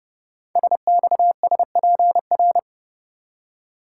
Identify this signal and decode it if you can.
Morse code, heard as a keyed tone, about 30 wpm, 700 Hz. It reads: SXSPR
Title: Code Groups - 5 Letters - 30wpm